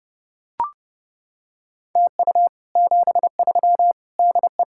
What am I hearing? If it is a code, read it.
Morse code, heard as a keyed tone, about 30 wpm, 700 Hz. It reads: TU 73 DE